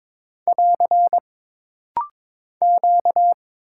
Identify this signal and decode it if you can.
Morse code, heard as a keyed tone, about 22 wpm, 700 Hz. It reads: <AR> Q
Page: This rapid-fire timing is geared to push your ICR (Instant Character Recognition) skills.